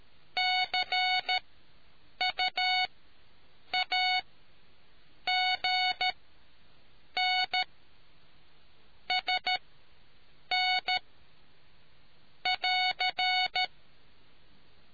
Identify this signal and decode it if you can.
Morse code, heard as a keyed tone, about 13 wpm, spaced as for 7 wpm, 740 Hz. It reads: CUAGN SN <AR>